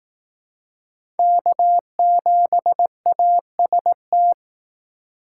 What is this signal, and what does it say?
Morse code, heard as a keyed tone, about 18 wpm, 700 Hz. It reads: K7AST
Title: Speed Racing - USA 1x3 Call Signs 15wpm